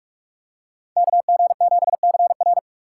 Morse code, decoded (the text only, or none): KG7CR